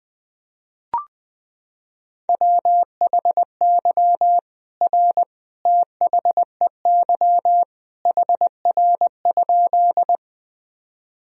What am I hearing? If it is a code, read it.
Morse code, heard as a keyed tone, about 20 wpm, 700 Hz. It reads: WHY R THEY HR?
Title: Sentences from Top 200 Words and Abbreviations 20wpm